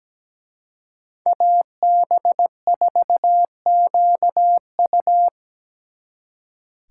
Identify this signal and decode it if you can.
Morse code, heard as a keyed tone, about 20 wpm, 700 Hz. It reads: AB4QU
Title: Speed Racing - USA 2x2 Call Signs 17wpm